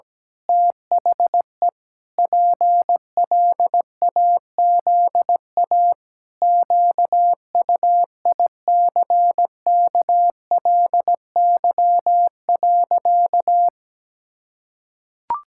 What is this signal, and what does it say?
Morse code, heard as a keyed tone, about 17 wpm, 700 Hz. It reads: THE PLAZA QUICKLY.
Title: Pangram Challenge v2 - 17wpm